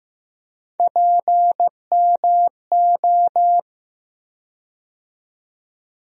Morse code, heard as a keyed tone, about 15 wpm, 700 Hz. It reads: PMO
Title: USA Call Sign Postfixes - 3 Letters 15wpm